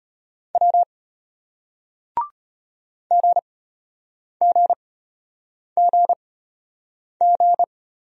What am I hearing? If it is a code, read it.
Morse code, heard as a keyed tone, about 33 wpm, spaced as for 8 wpm, 700 Hz. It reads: W GGGG